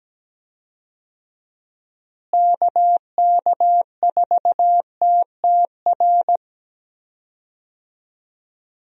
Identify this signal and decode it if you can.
Morse code, heard as a keyed tone, about 17 wpm, 700 Hz. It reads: KK4TTR
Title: USA Call Signs - 6 Characters - Repeated 2x - 17wpm